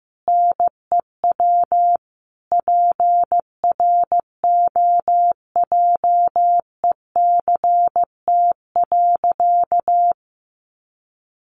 Morse code, heard as a keyed tone, about 15 wpm, 700 Hz. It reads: NEW PROJECT.
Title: Sentences from Top 1400 Words 15wpm